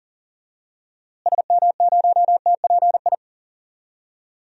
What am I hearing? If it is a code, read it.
Morse code, heard as a keyed tone, about 40 wpm, 700 Hz. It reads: SM0TPI
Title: DXCC-100 Call Signs - 6 Characters - Repeated 2x - 40wpm